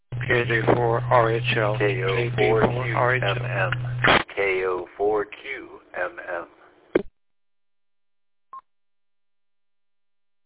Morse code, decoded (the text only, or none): none